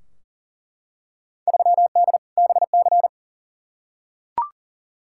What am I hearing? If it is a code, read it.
Morse code, heard as a keyed tone, about 40 wpm, 700 Hz. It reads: 3D BC
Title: Field Day Exchanges Encore - 40wpm